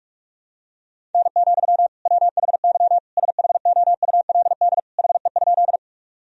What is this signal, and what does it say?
Morse code, heard as a keyed tone, about 46 wpm, 700 Hz. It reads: N, WHY SHOULD HE?
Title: Sentences from Top 300 Words and Abbreviations 45wpm